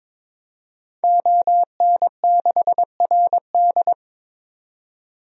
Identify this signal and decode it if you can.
Morse code, heard as a keyed tone, about 22 wpm, 700 Hz. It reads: ON6RD
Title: DXCC-100 Call Signs - 5 Characters - 22wpm